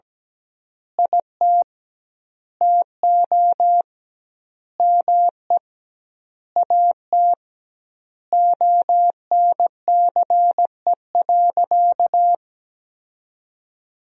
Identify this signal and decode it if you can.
Morse code, heard as a keyed tone, about 17 wpm, 700 Hz. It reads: IT TO ME AT ONCE.
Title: 2x Word-Spacing - Sentences from Top 500 Words - 17wpm